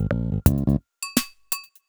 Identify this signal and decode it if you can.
There is no code.